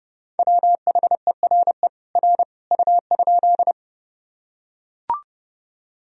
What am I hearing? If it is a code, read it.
Morse code, heard as a keyed tone, about 30 wpm, 700 Hz. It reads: WHERE R U?
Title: Sentences from Top 200 Words and Abbreviations 30wpm